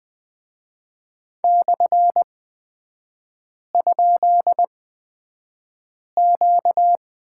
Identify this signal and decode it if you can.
Morse code, heard as a keyed tone, about 20 wpm, spaced as for 5 wpm, 700 Hz. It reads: /?Q
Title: 236 Lesson 38e - Letter q - Three Characters 20wpm